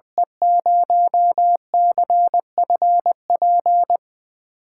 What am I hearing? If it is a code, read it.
Morse code, heard as a keyed tone, about 20 wpm, 700 Hz. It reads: E0CFP
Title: USA Call Signs - 8x Character Spacing - 20wpm